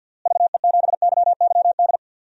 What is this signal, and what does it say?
Morse code, heard as a keyed tone, about 50 wpm, 700 Hz. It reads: VE7YYD
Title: DXCC-100 Call Signs - 6 Characters - Repeated 2x - 50wpm